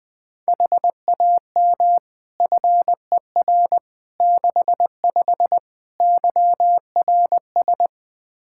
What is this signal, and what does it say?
Morse code, heard as a keyed tone, about 20 wpm, 700 Hz. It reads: HAM FER 65 YRS
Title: Practice QSO - Exchange 3 - More Info - 20wpm